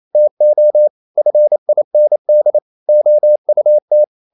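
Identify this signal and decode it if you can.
Morse code, heard as a keyed tone, about 28 wpm, 600 Hz. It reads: TO FIND OUT
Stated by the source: Morse code practice at 28 wpm, text of Ken Follett's novel Never in English, from CW med Gnister.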